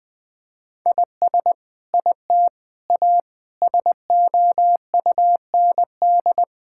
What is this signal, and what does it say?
Morse code, heard as a keyed tone, about 20 wpm, 700 Hz. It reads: IS IT A SOUND
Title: Sentences from Top 1000 Words - Review Part 5 - 20wpm